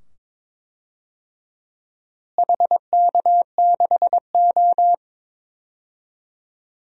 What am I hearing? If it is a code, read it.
Morse code, heard as a keyed tone, about 22 wpm, 700 Hz. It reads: HK6O
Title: DXCC-100 Call Signs - 4 Characters - Repeated 2x - 22wpm